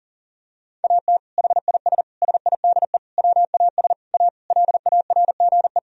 Morse code, heard as a keyed tone, about 40 wpm, 700 Hz. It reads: AT HIS SIDE WAS A LARGE
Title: Sentences from Top 1000 Words - Review Part 2 - 40wpm